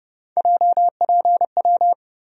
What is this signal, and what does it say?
Morse code, heard as a keyed tone, about 30 wpm, 700 Hz. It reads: JPW
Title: USA Call Sign Postfixes - 3 Letters 30wpm